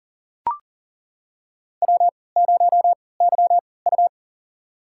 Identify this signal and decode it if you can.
Morse code, heard as a keyed tone, about 40 wpm, spaced as for 23 wpm, 700 Hz. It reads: W0YU
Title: USA Call Signs - 3x Character Spacing - 40wpm